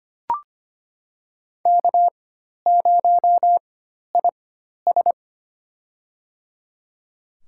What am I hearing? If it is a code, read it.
Morse code, heard as a keyed tone, about 25 wpm, spaced as for 12 wpm, 700 Hz. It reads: K0IS
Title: USA Call Signs - 4x Character Spacing - 25wpm